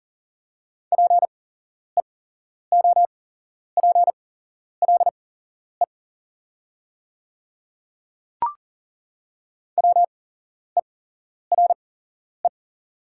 Morse code, heard as a keyed tone, about 41 wpm, spaced as for 11 wpm, 700 Hz. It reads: PEOPLE WERE
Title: 8x Character Spacing - Top 100 Words - 40wpm